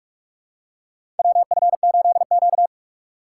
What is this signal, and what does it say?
Morse code, heard as a keyed tone, about 47 wpm, 700 Hz. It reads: WF8Q
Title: USA Call Signs - 4 Characters - 4x Character Spacing - 45wpm